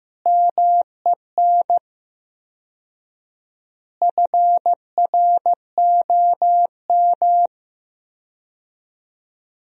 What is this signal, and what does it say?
Morse code, heard as a keyed tone, about 15 wpm, 700 Hz. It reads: MEN FROM
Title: 4x Word-Spacing - Sentences from Top 300 Words - 15wpm